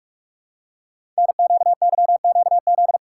Morse code, heard as a keyed tone, about 44 wpm, 690 Hz. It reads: NQYQZ
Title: Code Groups - 5 Letters - 45wpm